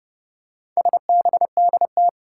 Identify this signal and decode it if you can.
Morse code, heard as a keyed tone, about 30 wpm, 700 Hz. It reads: SBDT
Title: Four Challenging Letter-Number Combinations 30wpm